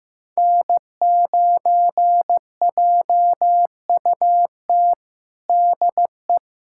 Morse code, heard as a keyed tone, about 15 wpm, 700 Hz. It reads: N9JUT DE
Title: USA Callsign DE Callsign - 5 Characters - 15wpm